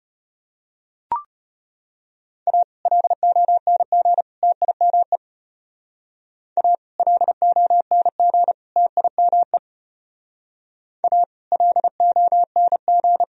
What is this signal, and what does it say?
Morse code, heard as a keyed tone, about 33 wpm, 700 Hz. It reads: A LONG TIME A LONG TIME A LONG